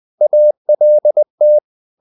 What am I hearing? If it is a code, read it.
Morse code, heard as a keyed tone, about 20 wpm, 600 Hz. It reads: ALT
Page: Ferie 20 wpm